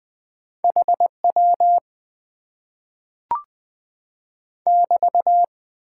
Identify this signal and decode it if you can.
Morse code, heard as a keyed tone, about 20 wpm, 700 Hz. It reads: H W <BT>
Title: Top 90 QSO Elements - 6x Character Spacing - 20wpm